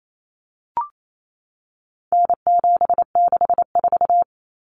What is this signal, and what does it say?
Morse code, heard as a keyed tone, about 28 wpm, 700 Hz. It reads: N764
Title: Common Cut Numbers - 4-Digit 28wpm